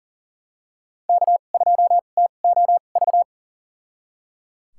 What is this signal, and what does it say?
Morse code, heard as a keyed tone, about 38 wpm, spaced as for 28 wpm, 700 Hz. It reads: K2TOV
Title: USA Call Signs - 5 Characters - 2x Character Spacing - 40wpm